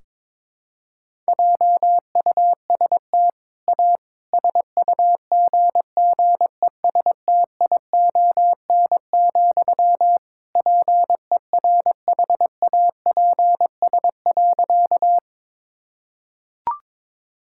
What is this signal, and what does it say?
Morse code, heard as a keyed tone, about 22 wpm, 700 Hz. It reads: JUST A SUGGESTION, PERHAPS.
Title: Sentences from Top 1600 Words 22wpm